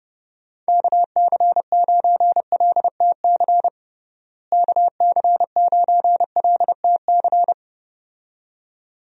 Morse code, heard as a keyed tone, about 32 wpm, 700 Hz. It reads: KC9LTC KC9LTC
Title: USA Call Signs - 6 Characters - Repeated 2x - 30wpm